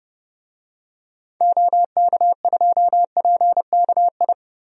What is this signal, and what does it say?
Morse code, heard as a keyed tone, about 30 wpm, 700 Hz. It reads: OK2PKI
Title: DXCC-100 Call Signs - 6 Characters - 30wpm